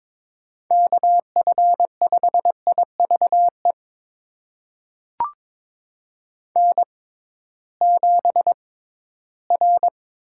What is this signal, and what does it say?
Morse code, heard as a keyed tone, about 22 wpm, spaced as for 14 wpm, 700 Hz. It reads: KF5IVE N7R
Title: USA Call Signs - 6x Character Spacing - 22wpm